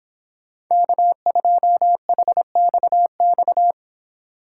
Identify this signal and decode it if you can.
Morse code, heard as a keyed tone, about 26 wpm, 700 Hz. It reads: K2HXX
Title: Speed Racing - USA 1x3 Call Signs 22wpm